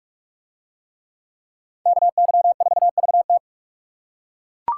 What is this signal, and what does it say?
Morse code, heard as a keyed tone, about 42 wpm, 700 Hz. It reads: KY4VT